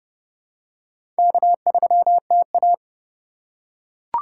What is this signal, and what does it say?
Morse code, heard as a keyed tone, about 32 wpm, 700 Hz. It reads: K3TA
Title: USA Call Signs - 4x Character Spacing - 30wpm